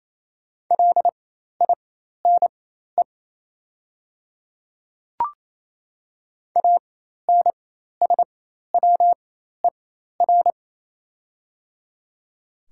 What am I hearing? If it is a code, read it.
Morse code, heard as a keyed tone, about 27 wpm, spaced as for 13 wpm, 690 Hz. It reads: LINE ANSWER
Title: 4x Character Spacing - Top 300 Words - 28wpm